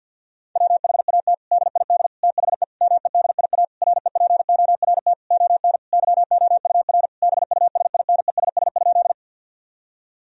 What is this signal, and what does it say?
Morse code, heard as a keyed tone, about 50 wpm, 700 Hz. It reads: WHAT DID THE MEDIA REPORT ON YOUR BUSINESS?
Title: Sentences from Top 1000 Words - Review Part 4 - 50wpm